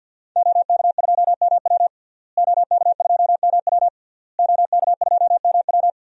Morse code, heard as a keyed tone, about 51 wpm, 700 Hz. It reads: OK2MW OK2MW OK2MW
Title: DXCC-100 Call Signs - 5 Characters - Repeated 3x - 50wpm